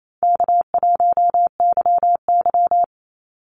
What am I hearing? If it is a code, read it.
Morse code, heard as a keyed tone, about 27 wpm, 700 Hz. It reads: K1YY
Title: Easy Call Signs Repeated 2x - 28wpm